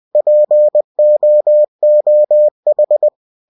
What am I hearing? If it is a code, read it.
Morse code, heard as a keyed tone, about 20 wpm, 600 Hz. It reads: POOH
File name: cw-books / winnie-the-pooh / winnie-20wpm-page-0070